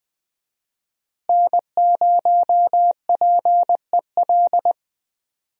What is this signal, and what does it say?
Morse code, heard as a keyed tone, about 20 wpm, 700 Hz. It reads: N0PEL
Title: USA Call Signs - 5 Characters - Repeated 3x - 20wpm